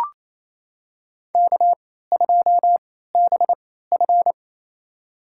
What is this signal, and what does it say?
Morse code, heard as a keyed tone, about 28 wpm, spaced as for 16 wpm, 700 Hz. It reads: K2BF
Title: USA Call Signs - 4 Characters - 3x Character Spacing - 28wpm